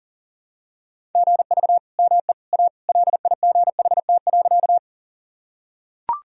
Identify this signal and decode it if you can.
Morse code, heard as a keyed tone, about 37 wpm, 700 Hz. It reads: GV ME A LIGHT.
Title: Sentences from Top 200 Words and Abbreviations 40wpm